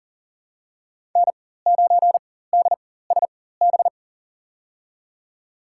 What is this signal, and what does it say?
Morse code, heard as a keyed tone, about 38 wpm, spaced as for 18 wpm, 700 Hz. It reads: N9DSB
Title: USA Call Signs - 5 Characters - 4x Character Spacing - 40wpm